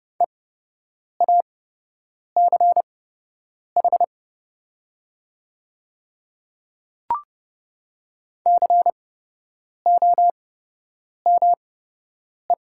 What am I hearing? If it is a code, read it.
Morse code, heard as a keyed tone, about 29 wpm, spaced as for 8 wpm, 700 Hz. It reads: EACH COME
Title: 8x Character Spacing - Top 100 Words - 30wpm